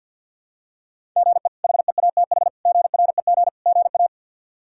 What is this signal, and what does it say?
Morse code, heard as a keyed tone, about 52 wpm, 700 Hz. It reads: GE HEATH GREG GA
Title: K1USN SST QSO Practice - 50wpm